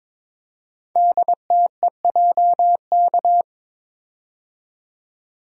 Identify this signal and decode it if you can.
Morse code, heard as a keyed tone, about 22 wpm, 700 Hz. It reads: DTEJK
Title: Code Groups - 5 Letters - 22wpm